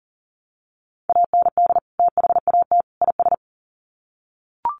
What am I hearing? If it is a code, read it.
Morse code, heard as a keyed tone, about 38 wpm, 700 Hz. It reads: AND THAT IS
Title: Sets of 3 Words 40wpm